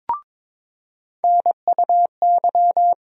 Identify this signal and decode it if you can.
Morse code, heard as a keyed tone, about 22 wpm, 700 Hz. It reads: NUY
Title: USA Call Sign Postfixes - 3 Letters 22wpm